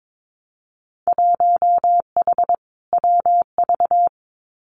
Morse code, heard as a keyed tone, about 22 wpm, 700 Hz. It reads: 1H WV
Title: Winter Field Day Exchanges 22wpm